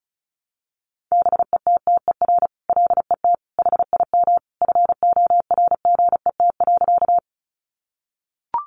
Morse code, heard as a keyed tone, about 34 wpm, 700 Hz. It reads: BETTER LET HIM FORGET.
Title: Sentences from Top 1200 Words 35wpm